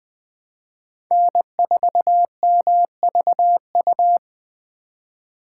Morse code, heard as a keyed tone, about 20 wpm, 700 Hz. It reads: N4MVU